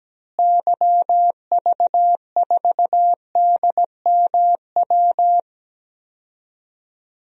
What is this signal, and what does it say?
Morse code, heard as a keyed tone, about 17 wpm, 700 Hz. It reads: YV4DMW
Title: DXCC-100 Call Signs - 6 Characters - Repeated 2x - 17wpm